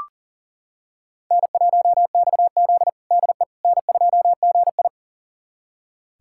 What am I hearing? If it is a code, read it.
Morse code, heard as a keyed tone, about 40 wpm, 700 Hz. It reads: N1XZ DE N2GI
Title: USA Callsign DE Callsign - 4 Characters - 40wpm